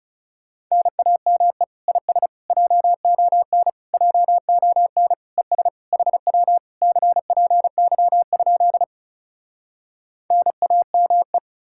Morse code, heard as a keyed tone, about 40 wpm, 700 Hz. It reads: NAME IS JON JON ES HW CPY? NAME